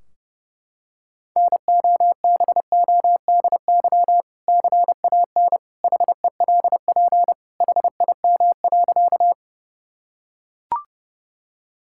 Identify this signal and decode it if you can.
Morse code, heard as a keyed tone, about 30 wpm, 700 Hz. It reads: NOBODY CAN HELP HIM.
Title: Sentences from Top 1600 Words 30wpm